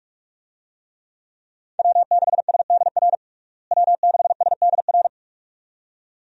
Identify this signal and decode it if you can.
Morse code, heard as a keyed tone, about 43 wpm, 700 Hz. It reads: W6SDR W6SDR
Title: USA Call Signs - 5 Characters - Repeated 2x - 45wpm